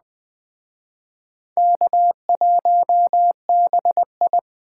Morse code, heard as a keyed tone, about 21 wpm, 700 Hz. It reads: K1BI